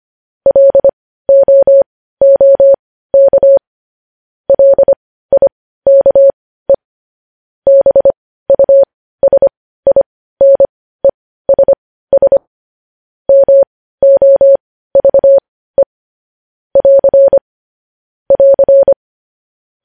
Morse code, heard as a keyed tone, about 25 wpm, spaced as for 15 wpm, 550 Hz. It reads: LOOK LIKE BUSINESS MOVE <AR> <AR>
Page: news headlines in morse code at 15 wpm